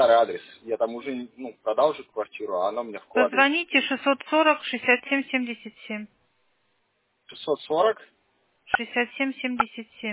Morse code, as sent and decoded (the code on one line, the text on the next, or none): none